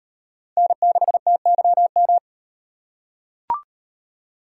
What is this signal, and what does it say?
Morse code, heard as a keyed tone, about 38 wpm, 700 Hz. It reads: N6TYM